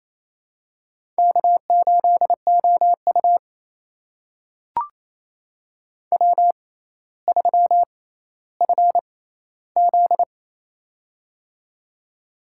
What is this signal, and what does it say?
Morse code, heard as a keyed tone, about 28 wpm, spaced as for 14 wpm, 700 Hz. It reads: K8OU W3FZ
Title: USA Call Signs - 4 Characters - 6x Character Spacing - 28wpm